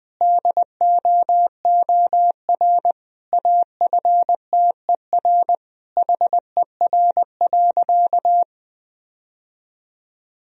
Morse code, heard as a keyed tone, about 20 wpm, 700 Hz. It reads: DOOR AFTER HER.
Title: Sentences from Top 1000 Words - Review Part 1 - 20wpm